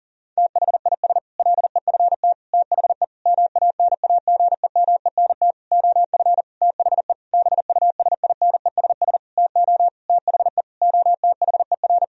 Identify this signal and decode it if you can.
Morse code, heard as a keyed tone, about 40 wpm, 700 Hz. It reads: THIS LEFT THE MANAGEMENT OF THE BUSINESS TO THE OTHER
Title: Sentences from Top 1000 Words - Review Part 4 - 40wpm